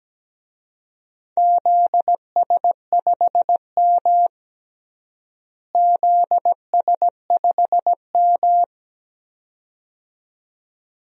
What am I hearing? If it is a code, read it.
Morse code, heard as a keyed tone, about 17 wpm, 700 Hz. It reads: ZS5M ZS5M
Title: DXCC-100 Call Signs - 4 Characters - Repeated 2x - 17wpm